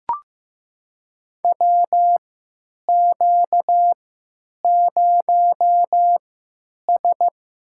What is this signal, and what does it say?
Morse code, heard as a keyed tone, about 15 wpm, spaced as for 9 wpm, 700 Hz. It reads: WQ0S
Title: USA Call Signs - 4 Characters - 3x Character Spacing - 15wpm